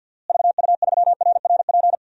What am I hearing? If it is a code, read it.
Morse code, heard as a keyed tone, about 50 wpm, 700 Hz. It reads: VU3RRP
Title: DXCC-100 Call Signs - 6 Characters - Repeated 2x - 50wpm